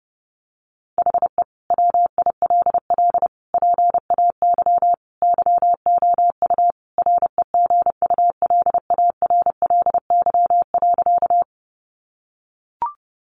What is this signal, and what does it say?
Morse code, heard as a keyed tone, about 30 wpm, 700 Hz. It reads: HE WILL PAY YOU REGULARLY.
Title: Sentences from Top 1400 Words 30wpm